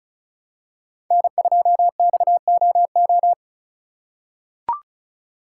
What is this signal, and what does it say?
Morse code, heard as a keyed tone, about 36 wpm, 700 Hz. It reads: N2XOO
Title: USA Call Signs - 5 Characters - 4x Character Spacing - 35wpm